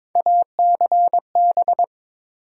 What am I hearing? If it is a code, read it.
Morse code, heard as a keyed tone, about 22 wpm, 700 Hz. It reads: ACB
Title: USA Call Sign Postfixes - 3 Letters 22wpm